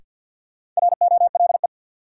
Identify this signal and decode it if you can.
Morse code, heard as a keyed tone, about 52 wpm, 700 Hz. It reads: ROLE
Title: Sentences from Top 1500 Words - New Word Drill Alt - 50wpm